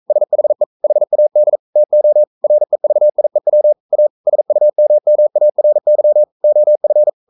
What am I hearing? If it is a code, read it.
Morse code, heard as a keyed tone, about 42 wpm, 600 Hz. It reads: SHE HAD TO REVIEW A SUMMARY OF
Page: Romanen Never af Ken Follet på engelsk.